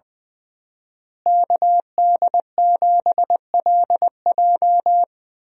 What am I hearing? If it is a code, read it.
Morse code, heard as a keyed tone, about 20 wpm, 700 Hz. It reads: KD7LJ
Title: USA Call Signs - 5 Characters - Repeated 3x - 20wpm